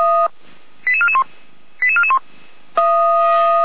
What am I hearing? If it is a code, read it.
Five-tone selective call (ZVEI), heard as quick runs of five tones, tones of about 80 ms. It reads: 88441 88441
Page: /455/SDR/ZVEI/